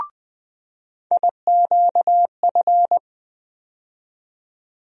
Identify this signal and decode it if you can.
Morse code, heard as a keyed tone, about 20 wpm, 700 Hz. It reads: IQF
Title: USA Call Sign Postfixes - 3 Letters 20wpm